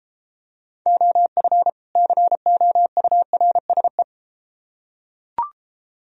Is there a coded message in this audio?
Yes: Morse code from 0.9 to 4.0 s, heard as a keyed tone, about 33 wpm, 700 Hz. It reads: OF COURSE